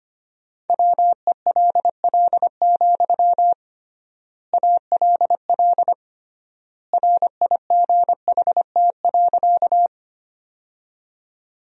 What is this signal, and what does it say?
Morse code, heard as a keyed tone, about 25 wpm, 700 Hz. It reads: WELL, ALL RIGHT.
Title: 3x Word-Spacing - Sentences from Top 200 Words - 25wpm